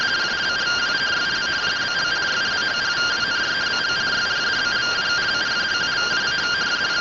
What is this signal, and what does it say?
Two-tone teleprinter signal, about 170 Hz shift.
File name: findings / MR CORONAL fr1 -131.gen-17 rtty rx